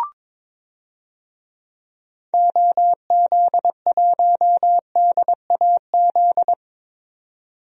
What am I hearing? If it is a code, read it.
Morse code, heard as a keyed tone, about 22 wpm, 700 Hz. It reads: OZ1DAZ
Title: DXCC-100 Call Signs - 6 Characters - 22wpm